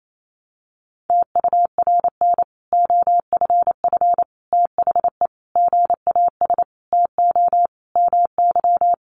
Morse code, heard as a keyed tone, about 28 wpm, 700 Hz. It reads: TURN OFF THE GAS TO MY
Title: Sentences from Top 1100 Words 28wpm